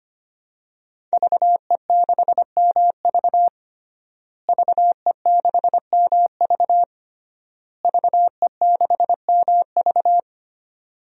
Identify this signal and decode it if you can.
Morse code, heard as a keyed tone, about 25 wpm, 700 Hz. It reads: VE6MV VE6MV VE6MV
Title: DXCC-100 Call Signs - 5 Characters - Repeated 3x - 25wpm